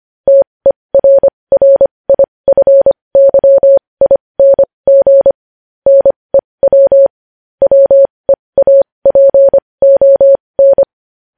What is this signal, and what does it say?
Morse code, heard as a keyed tone, about 25 wpm, spaced as for 20 wpm, 550 Hz. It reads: TERRIFYING NEW WEAPON
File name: podcast / News headlines-1x-25x20